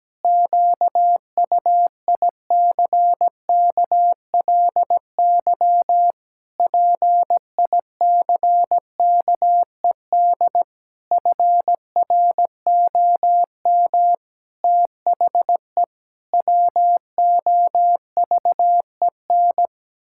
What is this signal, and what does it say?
Morse code, heard as a keyed tone, about 17 wpm, 700 Hz. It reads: QUICKLY PICKED FROM THE WOVEN
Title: Pangram Challenge v2 - 17wpm